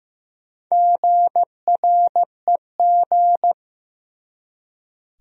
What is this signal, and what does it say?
Morse code, heard as a keyed tone, about 15 wpm, 700 Hz. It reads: GREG
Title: USA Common Names - Max 4 Letters - 15wpm